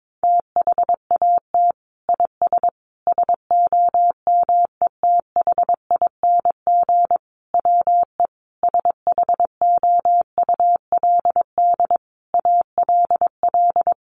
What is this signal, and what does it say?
Morse code, heard as a keyed tone, about 22 wpm, 700 Hz. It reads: THAT IS SOMETHING WE SHOULD ALL
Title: Sentences from Top 1100 Words 22wpm